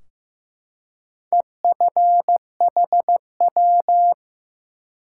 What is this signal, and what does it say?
Morse code, heard as a keyed tone, about 15 wpm, 700 Hz. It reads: EFHW
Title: Top 180 QSO Elements 15wpm